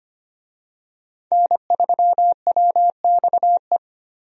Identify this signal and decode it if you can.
Morse code, heard as a keyed tone, about 25 wpm, 700 Hz. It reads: N3WXE
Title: USA Call Signs - 5 Characters - 25wpm